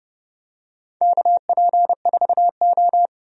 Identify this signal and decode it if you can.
Morse code, heard as a keyed tone, about 30 wpm, 700 Hz. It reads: KP4O
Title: DXCC-100 Call Signs - 4 Characters - 30wpm